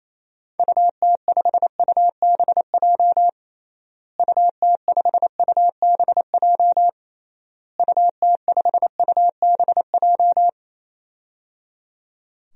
Morse code, heard as a keyed tone, about 28 wpm, 700 Hz. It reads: UT5UBJ UT5UBJ UT5UBJ
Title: DXCC-100 Call Signs - 6 Characters - Repeated 3x - 28wpm